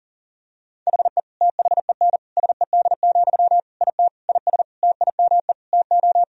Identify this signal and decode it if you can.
Morse code, heard as a keyed tone, about 40 wpm, 700 Hz. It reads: HE THEN SED, IT IS TIME TO
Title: Sentences from Top 300 Words and Abbreviations 40wpm